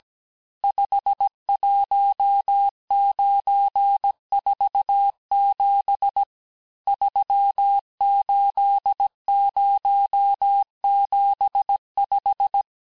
Morse code, heard as a keyed tone, about 17 wpm, 800 Hz. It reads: 51947 38075
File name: corrige-cours-1581-texte-chiffre-vitesse-16-mpm